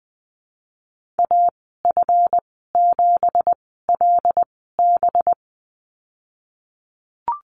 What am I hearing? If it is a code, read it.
Morse code, heard as a keyed tone, about 20 wpm, spaced as for 15 wpm, 700 Hz. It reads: AF7LB
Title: 193 Lesson 31g - Letter v - Call Signs 20wpm